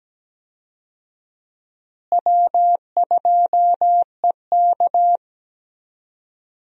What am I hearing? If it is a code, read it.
Morse code, heard as a keyed tone, about 17 wpm, 700 Hz. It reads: W2EK
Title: USA Call Signs - 4 Characters - Repeated 2x - 17wpm